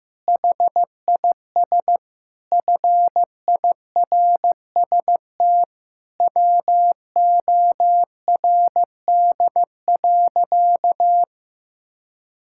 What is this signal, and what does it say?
Morse code, heard as a keyed tone, about 15 wpm, 700 Hz. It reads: HIS FIRST WORD.
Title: Pangram Challenge v2 - 15wpm